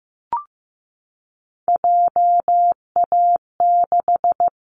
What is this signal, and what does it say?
Morse code, heard as a keyed tone, about 15 wpm, 700 Hz. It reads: JA6
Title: Callsign Prefixes for Top 50 DXCC - 15wpm